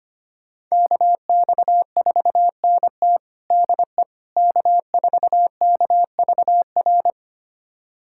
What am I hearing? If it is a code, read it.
Morse code, heard as a keyed tone, about 25 wpm, 700 Hz. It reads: KX4NT DE K4KVR
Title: USA Callsign DE Callsign - 5 Characters - 25wpm